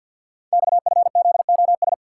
Morse code, heard as a keyed tone, about 50 wpm, 700 Hz. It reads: XFZOS
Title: Code Groups - 5 Letters - 50wpm